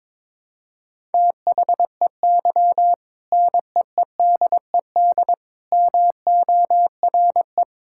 Morse code, heard as a keyed tone, about 21 wpm, 700 Hz. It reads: THEY NEEDED MORE
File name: practice / 22wpm / Sentences with 2-Word Phrases - 22wpm